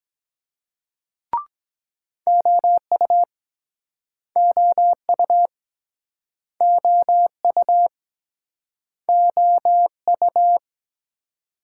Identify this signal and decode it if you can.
Morse code, heard as a keyed tone, about 21 wpm, 700 Hz. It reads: OU OU OU OU